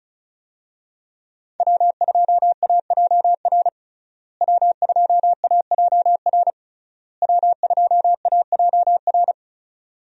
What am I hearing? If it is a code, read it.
Morse code, heard as a keyed tone, about 36 wpm, 700 Hz. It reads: W2AJR W2AJR W2AJR
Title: USA Call Signs - 5 Characters - Repeated 3x - 35wpm